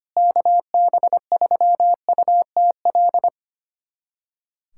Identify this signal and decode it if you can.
Morse code, heard as a keyed tone, about 25 wpm, 700 Hz. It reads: KB3UTL
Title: USA Callsign DE Callsign 25wpm